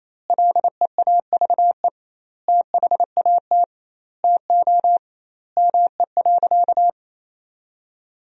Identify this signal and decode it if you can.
Morse code, heard as a keyed tone, about 28 wpm, 700 Hz. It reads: LEAVE THAT TO ME.
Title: 2x Word-Spacing - Sentences from Top 400 Words - 28wpm